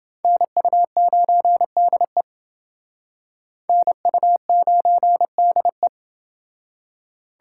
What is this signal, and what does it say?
Morse code, heard as a keyed tone, about 28 wpm, 700 Hz. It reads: NU9DE NU9DE